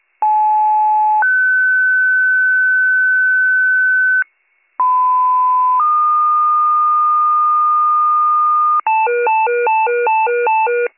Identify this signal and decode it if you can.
Two-tone page, heard as a one-second tone, then a three-second tone, about 860 then 1500 Hz, about 1010 then 1220 Hz, with noise. It is heard from 0.2 to 8.8 s.